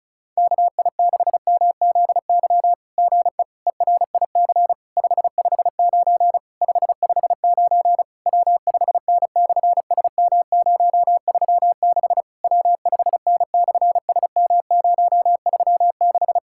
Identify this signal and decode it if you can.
Morse code, heard as a keyed tone, about 36 wpm, 700 Hz. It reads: KI6MZY GE ERIC 559 559 W5N/SM036 W5N/SM036
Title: SOTA QSO Practice 35wpm